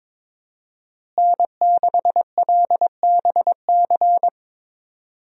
Morse code, heard as a keyed tone, about 22 wpm, 700 Hz. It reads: N6LBC
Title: USA Call Signs - 5 Characters - Repeated 3x - 22wpm